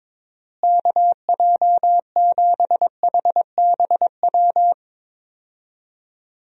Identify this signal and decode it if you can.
Morse code, heard as a keyed tone, about 22 wpm, 700 Hz. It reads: KJ7HBW
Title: USA Call Signs - 6 Characters - Repeated 3x - 22wpm